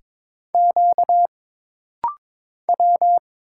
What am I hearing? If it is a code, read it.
Morse code, heard as a keyed tone, about 22 wpm, 700 Hz. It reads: Q W
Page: This rapid-fire timing is geared to push your ICR (Instant Character Recognition) skills.